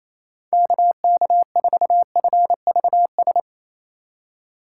Morse code, heard as a keyed tone, about 28 wpm, 700 Hz. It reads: KK4FVS
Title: USA Call Signs - 6 Characters - 8x Character Spacing - 28wpm